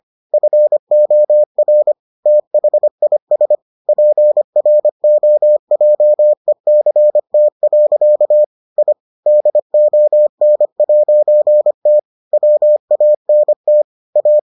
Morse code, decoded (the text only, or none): FOR THIS PROJECT. I DON'T WANT A